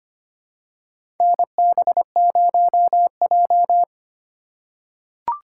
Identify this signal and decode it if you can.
Morse code, heard as a keyed tone, about 25 wpm, 700 Hz. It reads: NB0J
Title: USA Call Signs - 4 Characters - 3x Character Spacing - 25wpm